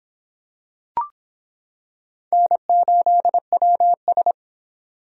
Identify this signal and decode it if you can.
Morse code, heard as a keyed tone, about 26 wpm, 700 Hz. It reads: N8WS